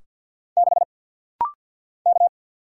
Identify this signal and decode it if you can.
Morse code, heard as a keyed tone, about 50 wpm, 690 Hz. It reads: 6 K
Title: Single Letter-Number Alt - Rapid-Fire 50wpm